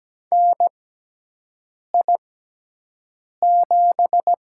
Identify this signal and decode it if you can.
Morse code, heard as a keyed tone, about 17 wpm, spaced as for 6 wpm, 700 Hz. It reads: NI7
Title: USA Call Signs - 4 Characters - 6x Character Spacing - 17wpm